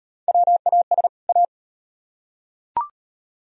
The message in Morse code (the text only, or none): WAS A